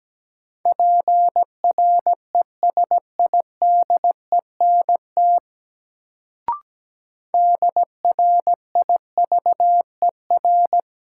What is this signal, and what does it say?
Morse code, heard as a keyed tone, about 17 wpm, 700 Hz. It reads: PRESIDENT DRIVER
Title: Sentences from Top 1000 Words - New Word Drill - 17wpm